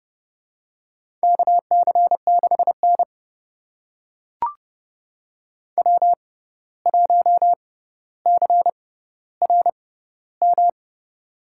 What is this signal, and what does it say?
Morse code, heard as a keyed tone, about 30 wpm, spaced as for 14 wpm, 700 Hz. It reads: KC6N W1CRM
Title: USA Call Signs - 6x Character Spacing - 30wpm